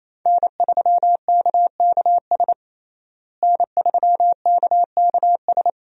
Morse code, heard as a keyed tone, about 28 wpm, 700 Hz. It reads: N3KKS N3KKS
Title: USA Call Signs - 5 Characters - Repeated 3x - 28wpm